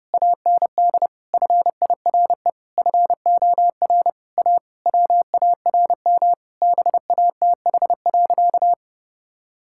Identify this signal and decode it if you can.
Morse code, heard as a keyed tone, about 30 wpm, 700 Hz. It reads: AND FIRE FOR A WARM BATH.
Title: Sentences from Top 1600 Words 30wpm